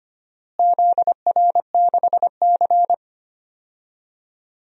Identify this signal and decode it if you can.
Morse code, heard as a keyed tone, about 25 wpm, 700 Hz. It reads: ZR6C
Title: DXCC-100 Call Signs - 4 Characters - Repeated 3x - 25wpm